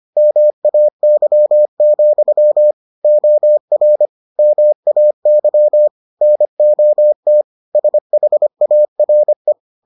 Morse code, heard as a keyed tone, about 25 wpm, 600 Hz. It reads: MAY, OR MAY NOT SHARE